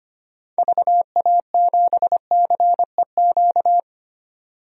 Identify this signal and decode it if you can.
Morse code, heard as a keyed tone, about 25 wpm, 700 Hz. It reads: VA7CEQ
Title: DXCC-100 Call Signs - 6 Characters - 25wpm